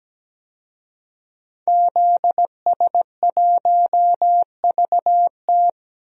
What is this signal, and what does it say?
Morse code, heard as a keyed tone, about 17 wpm, 700 Hz. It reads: ZS1VT
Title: DXCC-100 Call Signs - 5 Characters - 17wpm